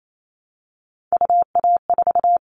Morse code, heard as a keyed tone, about 28 wpm, 700 Hz. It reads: UA4
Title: Callsign Prefixes for Top 100 DXCC - 28wpm